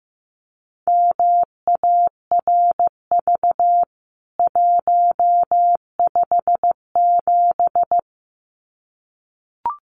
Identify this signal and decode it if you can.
Morse code, heard as a keyed tone, about 15 wpm, 700 Hz. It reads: MARV 157
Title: CWT Exchange 15wpm